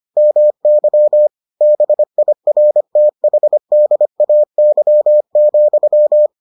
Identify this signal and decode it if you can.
Morse code, heard as a keyed tone, about 25 wpm, 600 Hz. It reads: MY BIRTHDAY,